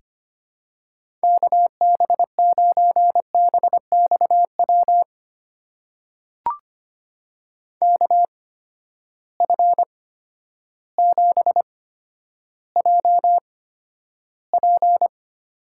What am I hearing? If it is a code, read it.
Morse code, heard as a keyed tone, about 25 wpm, spaced as for 11 wpm, 700 Hz. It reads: KB9BXW KF7JP
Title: USA Call Signs - 6 Characters - 8x Character Spacing - 25wpm